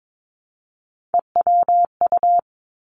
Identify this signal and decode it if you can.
Morse code, heard as a keyed tone, about 22 wpm, 700 Hz. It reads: EWU
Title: Three Challenging Letter-Number Combinations 22wpm